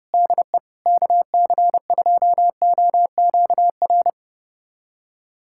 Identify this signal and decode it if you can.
Morse code, heard as a keyed tone, about 30 wpm, 700 Hz. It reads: DE KC2OQR
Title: USA Callsign DE Callsign 30wpm